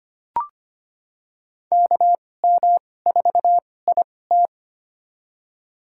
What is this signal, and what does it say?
Morse code, heard as a keyed tone, about 25 wpm, spaced as for 18 wpm, 700 Hz. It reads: KM4IT
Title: USA Call Signs - 5 Characters - 2x Character Spacing - 25wpm